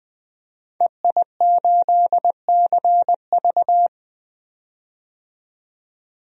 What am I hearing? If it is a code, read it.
Morse code, heard as a keyed tone, about 21 wpm, 700 Hz. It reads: EI8CV
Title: DXCC-100 Call Signs - 5 Characters - 20wpm